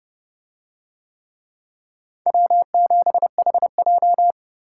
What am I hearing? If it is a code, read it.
Morse code, heard as a keyed tone, about 29 wpm, 700 Hz. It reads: W7HJ